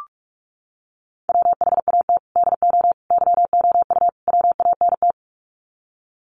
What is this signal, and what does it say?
Morse code, heard as a keyed tone, about 44 wpm, 700 Hz. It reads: WHAT DO YOU WANT
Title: Sets of 4 Words 45wpm